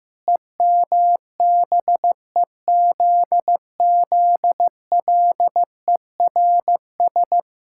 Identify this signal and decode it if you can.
Morse code, heard as a keyed tone, about 15 wpm, 700 Hz. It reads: EMBEZZLERS
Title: Pangram Challenge v2 - 15wpm